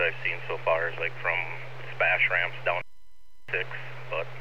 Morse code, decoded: none

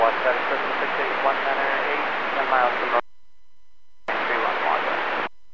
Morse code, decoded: none

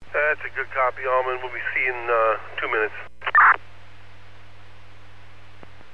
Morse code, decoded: none

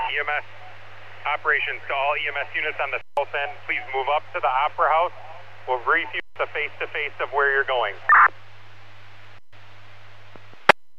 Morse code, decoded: none

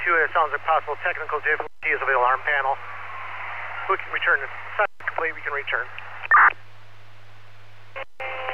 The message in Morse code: none

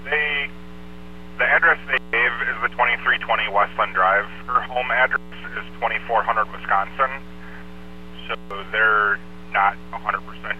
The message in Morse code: none

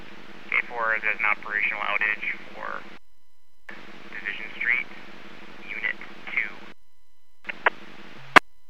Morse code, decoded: none

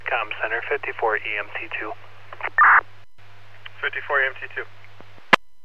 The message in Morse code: none

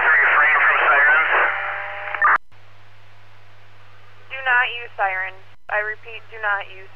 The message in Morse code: none

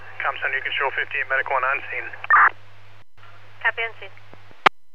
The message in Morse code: none

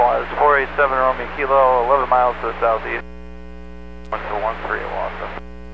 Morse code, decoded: none

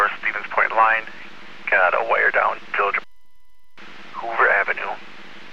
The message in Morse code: none

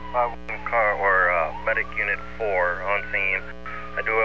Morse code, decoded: none

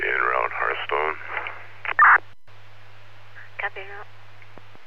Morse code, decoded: none

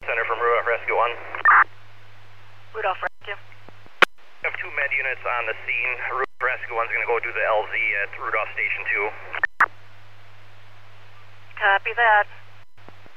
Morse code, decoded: none

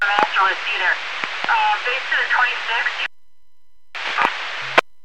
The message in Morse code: none

